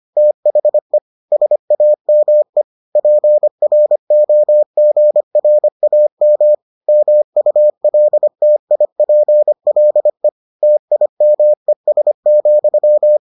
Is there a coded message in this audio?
Yes: Morse code from 0.2 to 13.2 s, heard as a keyed tone, about 25 wpm, 600 Hz. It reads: THE SAME PROGRAM MULTIPLE TIMES,